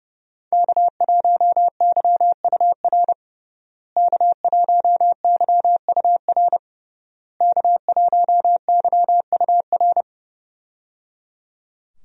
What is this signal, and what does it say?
Morse code, heard as a keyed tone, about 30 wpm, 700 Hz. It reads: K1YUR K1YUR K1YUR
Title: USA Call Signs - 5 Characters - Repeated 3x - 30wpm